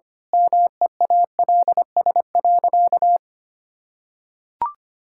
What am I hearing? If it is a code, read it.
Morse code, heard as a keyed tone, about 25 wpm, 700 Hz. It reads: MEALS.
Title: Pangram Challenge v2 - 25wpm